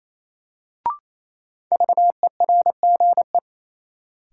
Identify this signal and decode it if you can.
Morse code, heard as a keyed tone, about 27 wpm, 700 Hz. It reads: VERGE
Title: Sets of 4 Words - New Word Drill - 28wpm